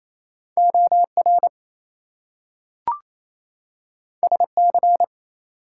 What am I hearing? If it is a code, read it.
Morse code, heard as a keyed tone, about 28 wpm, 700 Hz. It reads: OR SC
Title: US States and Canadian Provinces 28wpm